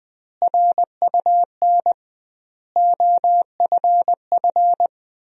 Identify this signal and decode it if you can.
Morse code, heard as a keyed tone, about 20 wpm, 700 Hz. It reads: RUN OFF
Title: 2x Word-Spacing - Sentences from Top 300 Words - 20wpm